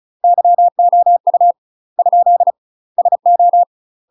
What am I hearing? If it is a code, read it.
Morse code, heard as a keyed tone, about 34 wpm, 700 Hz. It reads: YOU ? SO
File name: UK MASTER QSOs TRAINING / 35WPM / UK MASTER QSOs 0000